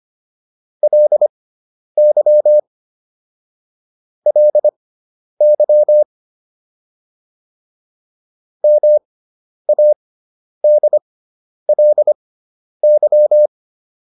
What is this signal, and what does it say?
Morse code, heard as a keyed tone, about 25 wpm, spaced as for 10 wpm, 600 Hz. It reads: LY LY MADLY